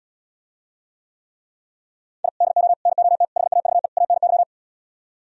Morse code, heard as a keyed tone, about 77 wpm, 700 Hz. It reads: I DO NOT HAVE ANY